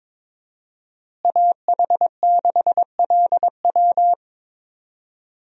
Morse code, heard as a keyed tone, about 22 wpm, 700 Hz. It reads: AH6LW
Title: USA Call Signs - 5 Characters - Repeated 2x - 22wpm